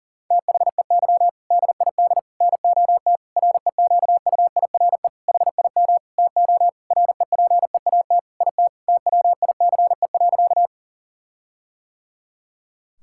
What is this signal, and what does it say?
Morse code, heard as a keyed tone, about 37 wpm, 700 Hz. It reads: THEY DID NOT REQUIRE HIM TO REPEAT IT TWICE.
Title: Sentences from Top 1600 Words 40wpm